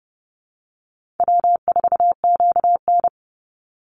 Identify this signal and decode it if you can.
Morse code, heard as a keyed tone, about 30 wpm, 700 Hz. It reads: W4QN
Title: Easy Call Signs Repeated 3x - 30wpm